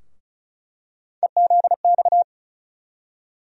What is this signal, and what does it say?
Morse code, heard as a keyed tone, about 36 wpm, 700 Hz. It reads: EZX
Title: USA Call Sign Postfixes - 3 Letters 35wpm